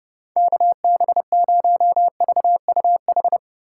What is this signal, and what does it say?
Morse code, heard as a keyed tone, about 30 wpm, 700 Hz. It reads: KB0VUH
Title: USA Call Signs - 6 Characters - 8x Character Spacing - 30wpm